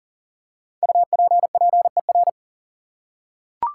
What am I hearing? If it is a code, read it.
Morse code, heard as a keyed tone, about 40 wpm, 700 Hz. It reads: UPPER
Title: Five Letter Words - 40wpm